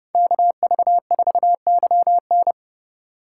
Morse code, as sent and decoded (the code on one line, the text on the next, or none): -.- ...- ....- -.-- -.
KV4YN